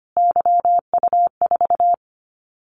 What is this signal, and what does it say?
Morse code, heard as a keyed tone, about 25 wpm, 700 Hz. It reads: YU4
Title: Callsign Prefixes for Top 25 DXCC - Repeated 3x - 25wpm